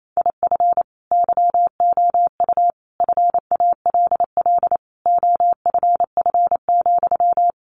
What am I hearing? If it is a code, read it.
Morse code, heard as a keyed tone, about 28 wpm, 700 Hz. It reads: IF YOU FALL OFF,
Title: A Sound Of Thunder 28wpm